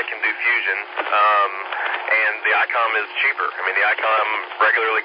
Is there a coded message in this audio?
Yes: Emergency Alert System (SAME) tones from 0.4 to 4.0 s, heard as harsh buzzing bursts, with speech.